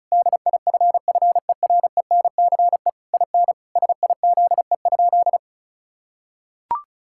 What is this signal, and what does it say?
Morse code, heard as a keyed tone, about 36 wpm, 700 Hz. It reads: DIFFERENCE IN SIZE?
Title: Sentences from Top 1000 Words - Review Part 5 - 35wpm